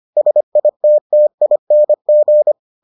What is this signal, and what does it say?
Morse code, heard as a keyed tone, about 25 wpm, 600 Hz. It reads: SITTING